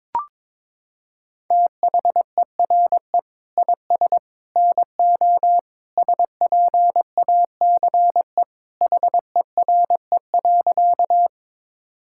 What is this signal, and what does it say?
Morse code, heard as a keyed tone, about 22 wpm, 700 Hz. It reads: THERE IS NO SPACE HERE.
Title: Sentences from Top 1000 Words - Review Part 5 - 22wpm